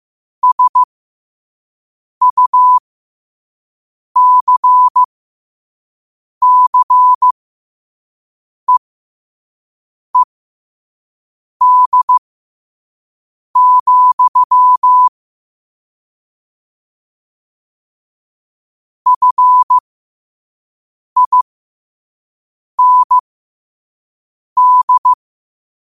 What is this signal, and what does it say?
Morse code, heard as a keyed tone, about 15 wpm, spaced as for 5 wpm, 1000 Hz. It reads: SUCCEED, FIND
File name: qotdfiles / 5wpm